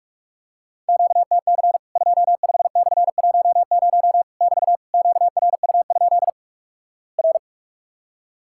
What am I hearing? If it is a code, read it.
Morse code, heard as a keyed tone, about 46 wpm, 700 Hz. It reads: QTC 25/10 <BT> QRU? R
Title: Worked All Europe DX Contest - QTC Exchange - 45wpm